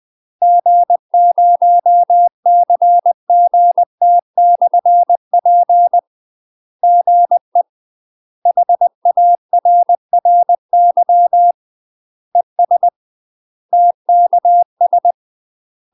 Morse code, heard as a keyed tone, about 20 wpm, 700 Hz. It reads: G0CGT/P GE HARRY ES TKS